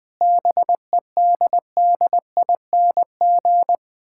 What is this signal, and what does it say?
Morse code, heard as a keyed tone, about 21 wpm, 690 Hz. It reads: BEDDING
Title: Difficult Words with Hints - 20wpm